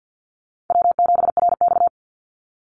Morse code, heard as a keyed tone, about 50 wpm, 700 Hz. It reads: W7RX
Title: Easy Call Signs Repeated 2x - 50wpm